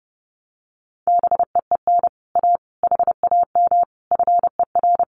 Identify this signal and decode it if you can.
Morse code, heard as a keyed tone, about 30 wpm, 700 Hz. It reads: BEEN A HAM FER
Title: Common QSO Phrases Encore 30wpm